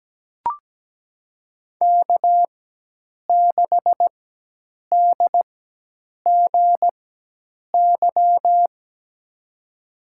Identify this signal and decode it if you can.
Morse code, heard as a keyed tone, about 17 wpm, spaced as for 8 wpm, 700 Hz. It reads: K6DGY
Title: USA Call Signs - 5 Characters - 4x Character Spacing - 17wpm